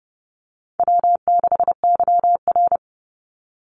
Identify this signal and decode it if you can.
Morse code, heard as a keyed tone, about 30 wpm, 700 Hz. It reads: W6YR
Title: Easy Call Signs Repeated 3x - 30wpm